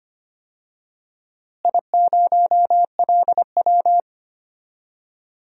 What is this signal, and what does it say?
Morse code, heard as a keyed tone, about 25 wpm, 700 Hz. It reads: I0LW
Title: DXCC-100 Call Signs - 4 Characters - Repeated 3x - 25wpm